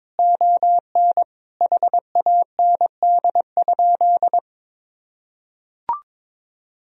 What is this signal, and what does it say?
Morse code, heard as a keyed tone, about 22 wpm, 700 Hz. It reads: ON HAND?
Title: Sentences from Top 1000 Words - Review Part 4 - 22wpm